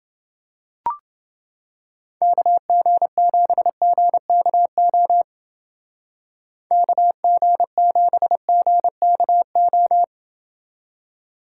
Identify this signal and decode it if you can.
Morse code, heard as a keyed tone, about 30 wpm, 700 Hz. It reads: KG7GKO KG7GKO